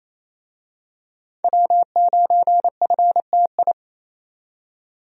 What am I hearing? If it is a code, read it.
Morse code, heard as a keyed tone, about 28 wpm, 700 Hz. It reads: W9FTI
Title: USA Call Signs - 5 Characters - 28wpm